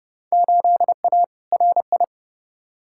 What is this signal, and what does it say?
Morse code, heard as a keyed tone, about 30 wpm, 700 Hz. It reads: 8A RI
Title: Field Day Exchanges Encore - 30wpm